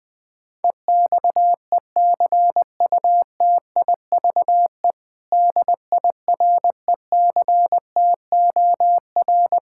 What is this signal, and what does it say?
Morse code, heard as a keyed tone, about 21 wpm, 700 Hz. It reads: EXECUTIVE DIRECTOR
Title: Build A Phrase - 4-Word Phrases - 20wpm